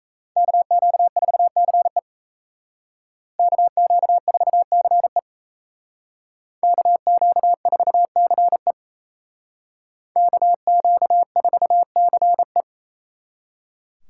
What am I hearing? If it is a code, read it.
Morse code, heard as a keyed tone, about 33 wpm, 700 Hz. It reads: KQ4CE KQ4CE KQ4CE KQ4CE
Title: Speed Racing - USA 2x2 Call Signs 28wpm